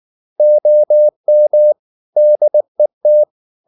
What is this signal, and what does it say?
Morse code, heard as a keyed tone, about 19 wpm, 600 Hz. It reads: OM DET